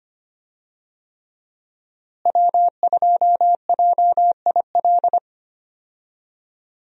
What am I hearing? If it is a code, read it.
Morse code, heard as a keyed tone, about 25 wpm, 700 Hz. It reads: W2JIL
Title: USA Call Signs - 5 Characters - Repeated 3x - 25wpm